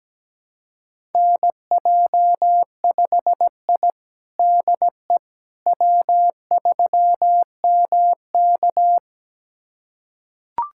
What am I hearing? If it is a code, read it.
Morse code, heard as a keyed tone, about 17 wpm, 700 Hz. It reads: NJ5I DE W3MK
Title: USA Callsign DE Callsign - 4 Characters - 17wpm